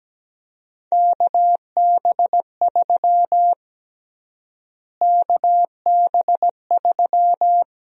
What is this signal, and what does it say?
Morse code, heard as a keyed tone, about 17 wpm, 700 Hz. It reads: KB3 KB3
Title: USA Call Sign Prefixes - Repeated 2x - 17wpm